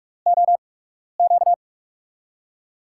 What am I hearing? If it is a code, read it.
Morse code, heard as a keyed tone, about 44 wpm, spaced as for 12 wpm, 700 Hz. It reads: OQ
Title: USA Call Signs - 5 Characters - 8x Character Spacing - 45wpm